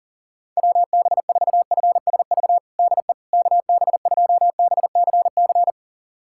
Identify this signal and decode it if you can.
Morse code, heard as a keyed tone, about 37 wpm, 690 Hz. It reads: WB4FSV DE KB2BCC
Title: USA Callsign DE Callsign - 6 Characters - 40wpm